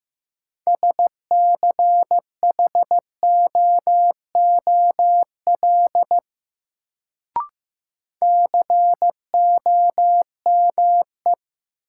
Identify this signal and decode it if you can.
Morse code, heard as a keyed tone, about 15 wpm, 700 Hz. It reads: SCHOOL COME
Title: Sets of 2 Words - New Word Drill - 15wpm